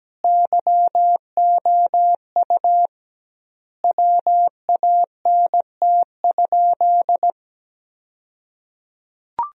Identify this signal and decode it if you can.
Morse code, heard as a keyed tone, about 17 wpm, 700 Hz. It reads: YOU WANT?
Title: 2x Word-Spacing - Sentences from Top 200 Words - 17wpm